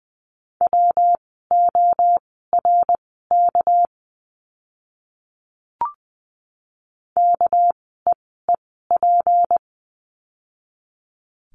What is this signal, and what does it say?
Morse code, heard as a keyed tone, about 20 wpm, spaced as for 15 wpm, 700 Hz. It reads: WORK KEEP
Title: 244 Lesson 39f - Letter z - Words 20wpm